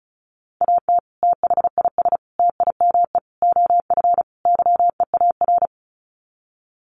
Morse code, heard as a keyed tone, about 34 wpm, 700 Hz. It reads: AT THIS TIME OF YEAR
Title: Sets of 5 Words Encore2 35wpm